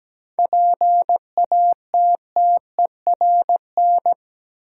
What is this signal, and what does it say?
Morse code, heard as a keyed tone, about 17 wpm, 700 Hz. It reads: PATTERN
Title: Seven Letter Words - 17wpm